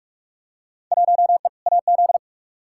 Morse code, heard as a keyed tone, about 46 wpm, 700 Hz. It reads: 1E AZ
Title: Field Day Exchanges - 45wpm